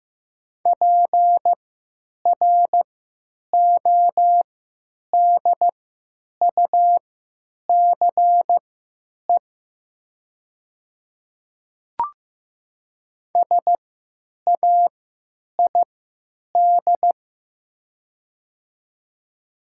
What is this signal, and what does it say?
Morse code, heard as a keyed tone, about 15 wpm, spaced as for 9 wpm, 700 Hz. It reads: PRODUCE SAID
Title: 3x Character Spacing - Top 500 Words - 15wpm